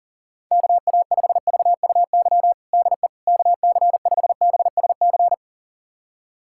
Morse code, heard as a keyed tone, about 40 wpm, 700 Hz. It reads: KA5VUY DE KC5BSC
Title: USA Callsign DE Callsign - 6 Characters - 40wpm